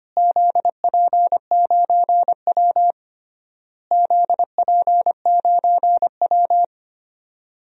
Morse code, heard as a keyed tone, about 25 wpm, 700 Hz. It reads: ZP9W ZP9W
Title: DXCC-100 Call Signs - 4 Characters - Repeated 3x - 25wpm